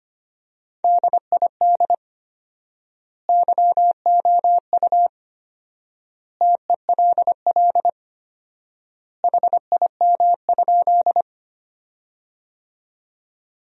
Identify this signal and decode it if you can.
Morse code, heard as a keyed tone, about 25 wpm, 700 Hz. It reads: DID YOU TELL HIM?
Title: 4x Word-Spacing - Sentences from Top 200 Words - 25wpm